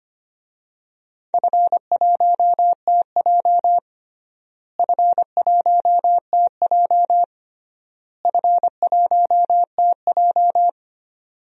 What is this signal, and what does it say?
Morse code, heard as a keyed tone, about 25 wpm, 700 Hz. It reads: F1TJ F1TJ F1TJ
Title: DXCC-100 Call Signs - 4 Characters - Repeated 3x - 25wpm